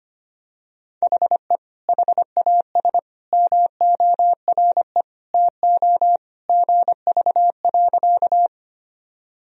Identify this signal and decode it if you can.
Morse code, heard as a keyed tone, about 25 wpm, 700 Hz. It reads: HE HAS MORE TO GV.
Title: Sentences from Top 200 Words and Abbreviations 25wpm